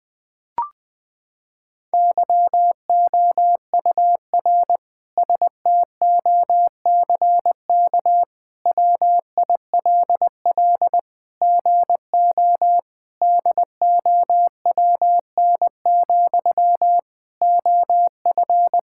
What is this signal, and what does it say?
Morse code, heard as a keyed tone, about 20 wpm, 700 Hz. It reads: YOUR STOCK WILL GO DOWN, OF
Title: Sentences from Top 1000 Words - Review Part 5 - 20wpm